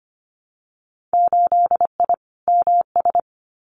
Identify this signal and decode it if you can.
Morse code, heard as a keyed tone, about 25 wpm, 700 Hz. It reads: 8I MS
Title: Winter Field Day Exchanges Encore 25wpm